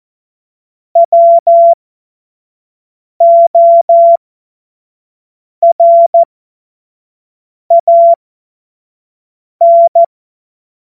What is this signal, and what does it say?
Morse code, heard as a keyed tone, about 14 wpm, spaced as for 5 wpm, 680 Hz. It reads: WORAN